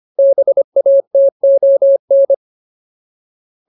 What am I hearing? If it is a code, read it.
Morse code, heard as a keyed tone, about 25 wpm, 550 Hz. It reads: BATON
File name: Ultimate Word List / 25WPM Ultimate